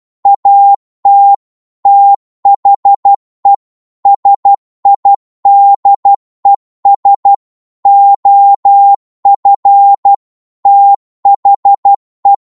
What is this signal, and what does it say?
Morse code, heard as a keyed tone, about 12 wpm, 800 Hz. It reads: AT THE SIDES OF THE
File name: cwmp3 / psg-1-12wpm